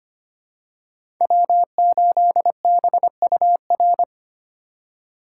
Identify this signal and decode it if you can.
Morse code, heard as a keyed tone, about 25 wpm, 700 Hz. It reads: W8BUR
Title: USA Call Signs - 2x Character Spacing - 25wpm